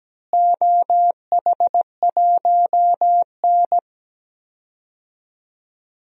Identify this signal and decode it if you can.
Morse code, heard as a keyed tone, about 17 wpm, 700 Hz. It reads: OH1N
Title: DXCC-100 Call Signs - 4 Characters - Repeated 3x - 17wpm